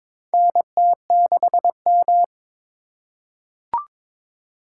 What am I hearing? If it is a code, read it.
Morse code, heard as a keyed tone, about 22 wpm, 700 Hz. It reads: NT6M
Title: USA Call Signs - 3x Character Spacing - 22wpm